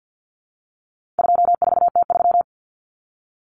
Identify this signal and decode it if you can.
Morse code, heard as a keyed tone, about 48 wpm, 700 Hz. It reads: 24T3
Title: Common Cut Numbers - 4-Digit 50wpm